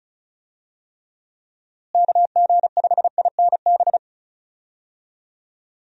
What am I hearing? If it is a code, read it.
Morse code, heard as a keyed tone, about 36 wpm, 700 Hz. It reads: KG5INB
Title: USA Call Signs - 6 Characters - Repeated 2x - 35wpm